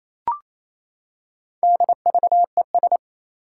Morse code, heard as a keyed tone, about 27 wpm, 700 Hz. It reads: DVES
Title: Four Challenging Letter-Number Combinations 28wpm